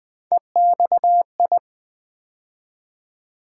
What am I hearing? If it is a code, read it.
Morse code, heard as a keyed tone, about 20 wpm, 700 Hz. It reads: EXI
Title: Build A Word - 3 Letters At A Time - 9-Letter Words - 20wpm